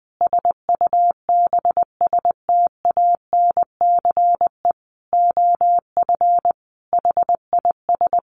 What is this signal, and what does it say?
Morse code, heard as a keyed tone, about 20 wpm, 700 Hz. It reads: SUBSTANCE OF HIS
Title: Sentences from Top 1500 Words 20wpm